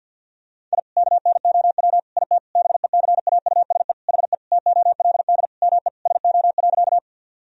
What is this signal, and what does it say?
Morse code, heard as a keyed tone, about 52 wpm, 700 Hz. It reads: I KNOW IT BECAUSE HE TOLD ME SO.
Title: Sentences from Top 1000 Words - Review Part 3 - 50wpm